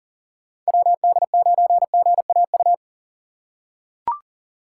Morse code, heard as a keyed tone, about 40 wpm, 700 Hz. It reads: WD9GAU
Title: USA Call Signs - 6x Character Spacing - 40wpm